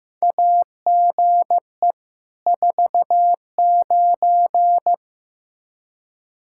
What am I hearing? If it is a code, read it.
Morse code, heard as a keyed tone, about 15 wpm, 700 Hz. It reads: AGE 49
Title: Practice QSO - Exchange 3 - More Info - 15wpm